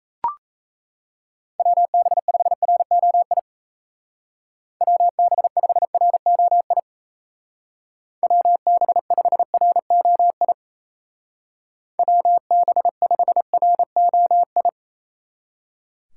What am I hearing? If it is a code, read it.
Morse code, heard as a keyed tone, about 36 wpm, 700 Hz. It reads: WB5ROI WB5ROI WB5ROI WB5ROI